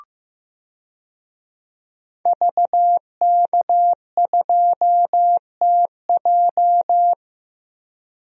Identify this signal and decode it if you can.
Morse code, heard as a keyed tone, about 15 wpm, 700 Hz. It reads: VK2TJ
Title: DXCC-100 Call Signs - 5 Characters - Repeated 2x - 15wpm